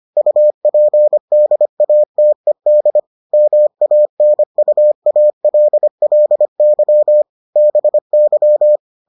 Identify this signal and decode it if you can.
Morse code, heard as a keyed tone, about 25 wpm, 600 Hz. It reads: UPDATED MANUALLY BY